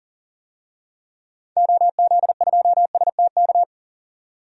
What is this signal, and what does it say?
Morse code, heard as a keyed tone, about 37 wpm, 700 Hz. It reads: OZ2STK
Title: DXCC-100 Call Signs - 6 Characters - 40wpm